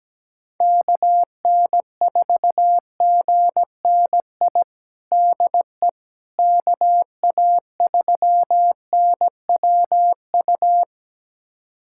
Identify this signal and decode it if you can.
Morse code, heard as a keyed tone, about 17 wpm, 700 Hz. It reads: KN4GNI DE KA3NWU
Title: USA Callsign DE Callsign - 6 Characters - 17wpm